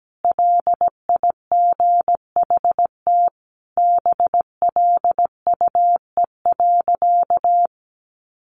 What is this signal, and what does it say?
Morse code, heard as a keyed tone, about 17 wpm, 700 Hz. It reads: LIGHT BLUE.
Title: Additional Easy Phrases 17wpm